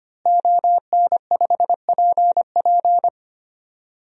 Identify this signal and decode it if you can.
Morse code, heard as a keyed tone, about 25 wpm, 700 Hz. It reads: ON5PP
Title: DXCC-100 Call Signs - 5 Characters - Repeated 2x - 25wpm